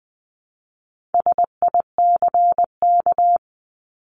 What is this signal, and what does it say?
Morse code, heard as a keyed tone, about 20 wpm, 700 Hz. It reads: SICK
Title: 260 Lesson 42e - Words 20-20wpm